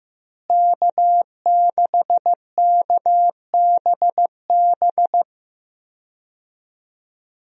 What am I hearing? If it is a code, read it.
Morse code, heard as a keyed tone, about 16 wpm, 700 Hz. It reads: K6KBB
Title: USA Call Signs - 5 Characters - Repeated 3x - 15wpm